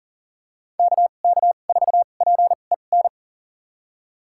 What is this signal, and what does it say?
Morse code, heard as a keyed tone, about 40 wpm, spaced as for 29 wpm, 700 Hz. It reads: KK4PEN
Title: USA Call Signs - 2x Character Spacing - 40wpm